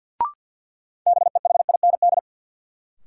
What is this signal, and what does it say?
Morse code, heard as a keyed tone, about 53 wpm, 700 Hz. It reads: BEHIND
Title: Sentences from Top 500 Words - New Word Drill - 50wpm